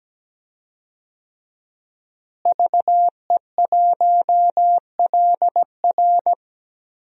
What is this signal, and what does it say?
Morse code, heard as a keyed tone, about 17 wpm, 700 Hz. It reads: VE1LR
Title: DXCC-100 Call Signs - 5 Characters - Repeated 2x - 17wpm